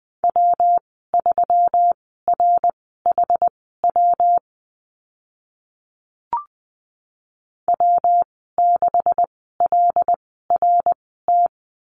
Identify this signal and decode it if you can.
Morse code, heard as a keyed tone, about 20 wpm, spaced as for 15 wpm, 700 Hz. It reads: W3RHW W6LRT
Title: 140 Lesson 23g - Letter f - Call Signs 20wpm